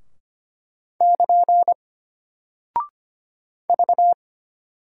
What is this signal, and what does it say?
Morse code, heard as a keyed tone, about 25 wpm, 700 Hz. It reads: <KN> V
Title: Single Character (Letter:Number:Punctuation:Prosign) Alt 25wpm